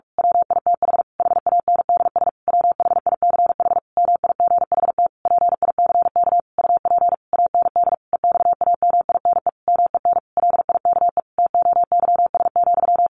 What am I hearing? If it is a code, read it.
Morse code, heard as a keyed tone, about 44 wpm, 700 Hz. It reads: WITH HANDS WHICH MIGHT PICK UP AND EXAMINE MEN LIKE TOYS,
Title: A Sound Of Thunder 45wpm